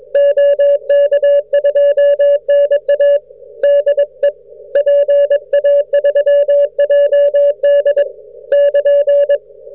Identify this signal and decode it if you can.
Morse code, heard as a keyed tone, about 21 wpm, 560 Hz. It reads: OK2NA DE PA3JD <KN>